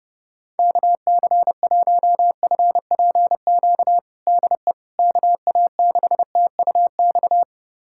Morse code, heard as a keyed tone, about 30 wpm, 700 Hz. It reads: KC1FPQ DE KA6TUX
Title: USA Callsign DE Callsign 30wpm